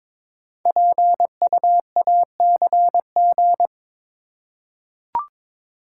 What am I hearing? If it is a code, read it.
Morse code, heard as a keyed tone, about 22 wpm, 700 Hz. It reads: PUACG
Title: Code Groups - 5 Letters - 22wpm